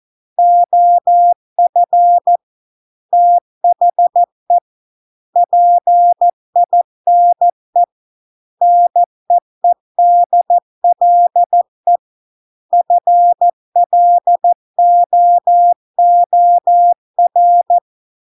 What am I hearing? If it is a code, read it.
Morse code, heard as a keyed tone, about 14 wpm, 700 Hz. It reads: OF THE PINE NEEDLE FLOOR